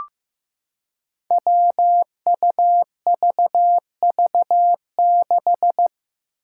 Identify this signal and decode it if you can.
Morse code, heard as a keyed tone, about 15 wpm, 700 Hz. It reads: WUVV6
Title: Five Challenging Letter-Number Combinations 15wpm